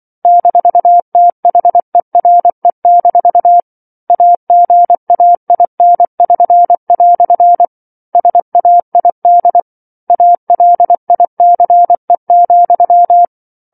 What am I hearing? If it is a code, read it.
Morse code, heard as a keyed tone, about 24 wpm, 700 Hz. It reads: -THERE- AGAIN<SN>" SAID ALICE,